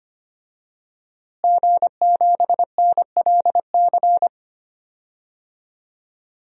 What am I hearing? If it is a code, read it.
Morse code, heard as a keyed tone, about 25 wpm, 700 Hz. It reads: G7NLC
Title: DXCC-100 Call Signs - 5 Characters - Repeated 2x - 25wpm